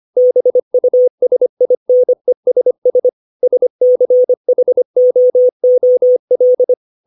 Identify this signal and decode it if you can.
Morse code, heard as a keyed tone, about 25 wpm, 500 Hz. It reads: BUSINESS SCHOOL
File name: 632 words / 632 Words 25WPM